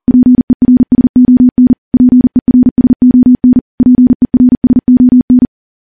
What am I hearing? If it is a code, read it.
Morse code, heard as a keyed tone, about 40 wpm, 250 Hz. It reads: PERSON PERSON PERSON
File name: morse / morse-training-40WPM